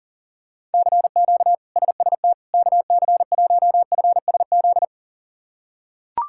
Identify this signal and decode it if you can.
Morse code, heard as a keyed tone, about 40 wpm, 700 Hz. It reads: CQ SST KC1FSZ
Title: K1USN SST QSO Practice - 40wpm